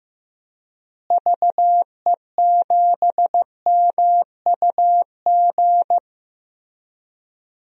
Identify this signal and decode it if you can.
Morse code, heard as a keyed tone, about 16 wpm, 700 Hz. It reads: VE7MUG
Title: DXCC-100 Call Signs - 6 Characters - Repeated 3x - 15wpm